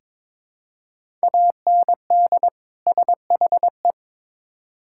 Morse code, heard as a keyed tone, about 22 wpm, 700 Hz. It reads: AND SHE
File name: Build A Phrase - 2-Word Phrases - 22wpm